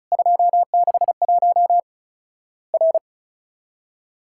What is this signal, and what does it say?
Morse code, heard as a keyed tone, about 36 wpm, 700 Hz. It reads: 261 R
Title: Worked All Europe DX Contest - QTC Exchange - 35wpm